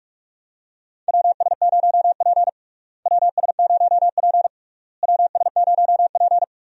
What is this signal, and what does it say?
Morse code, heard as a keyed tone, about 41 wpm, 700 Hz. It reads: WS0P WS0P WS0P
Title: USA Call Signs - 4 Characters - Repeated 3x - 45wpm